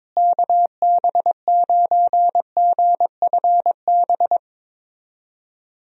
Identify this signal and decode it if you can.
Morse code, heard as a keyed tone, about 22 wpm, 700 Hz. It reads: KB9GFB
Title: USA Call Signs - 6 Characters - Repeated 2x - 22wpm